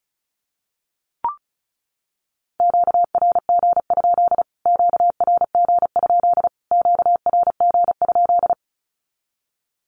Morse code, heard as a keyed tone, about 34 wpm, 700 Hz. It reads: QRG? QRG? QRG?
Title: Q-Codes Repeated 3x 35wpm